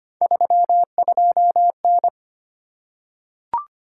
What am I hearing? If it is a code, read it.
Morse code, heard as a keyed tone, about 25 wpm, 700 Hz. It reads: 32N
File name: Three Letter-Number Combinations - 25wpm